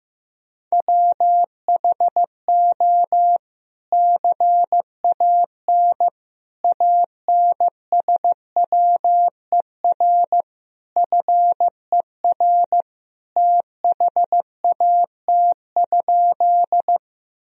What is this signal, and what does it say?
Morse code, heard as a keyed tone, about 15 wpm, 700 Hz. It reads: WHO CAN ANSWER FER THAT?
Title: Sentences from Top 300 Words and Abbreviations 15wpm